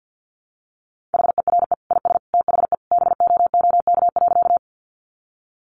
Morse code, heard as a keyed tone, about 50 wpm, 700 Hz. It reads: HERE IS THE BOOK.
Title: Sentences from Top 300 Words 50wpm